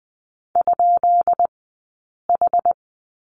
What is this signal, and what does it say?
Morse code, heard as a keyed tone, about 20 wpm, 700 Hz. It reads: ? H
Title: 183 Lesson 30d - Letter b - Two Characters 20wpm